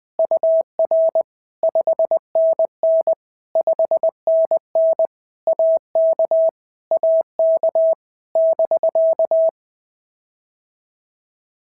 Morse code, heard as a keyed tone, about 20 wpm, 650 Hz. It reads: UR 5NN 5NN AK AK <BK>
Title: POTA QSO Practice - 20wpm